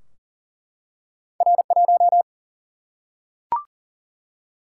Morse code, heard as a keyed tone, about 40 wpm, 700 Hz. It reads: R1
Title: Single-Letter Single-Number 40wpm